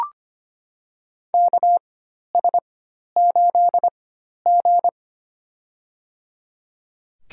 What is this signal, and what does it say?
Morse code, heard as a keyed tone, about 25 wpm, spaced as for 12 wpm, 700 Hz. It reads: KS8G
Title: USA Call Signs - 4 Characters - 4x Character Spacing - 25wpm